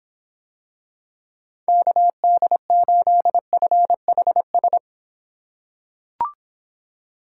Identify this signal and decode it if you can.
Morse code, heard as a keyed tone, about 25 wpm, 700 Hz. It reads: KD8FHS